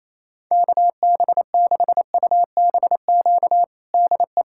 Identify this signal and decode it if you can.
Morse code, heard as a keyed tone, about 28 wpm, 700 Hz. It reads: KB6UBQ DE
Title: USA Callsign DE Callsign - 6 Characters - 28wpm